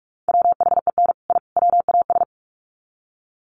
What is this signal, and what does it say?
Morse code, heard as a keyed tone, about 44 wpm, 700 Hz. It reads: WHEN I WAS
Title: Sets of 3 Words 45wpm